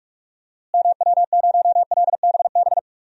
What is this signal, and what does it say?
Morse code, heard as a keyed tone, about 42 wpm, 700 Hz. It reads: MW0LBB
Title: DXCC-100 Call Signs - 6 Characters - Repeated 2x - 45wpm